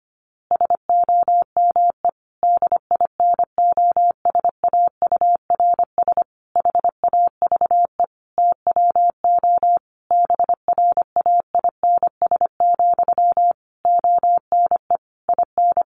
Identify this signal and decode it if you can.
Morse code, heard as a keyed tone, about 25 wpm, 700 Hz. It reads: SOME DINOSAURS HAVE TWO BRAINS, ONE IN
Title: A Sound Of Thunder 25wpm